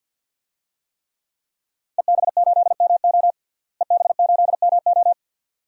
Morse code, heard as a keyed tone, about 52 wpm, 700 Hz. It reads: EB8MO EB8MO
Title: DXCC-100 Call Signs - 5 Characters - Repeated 2x - 50wpm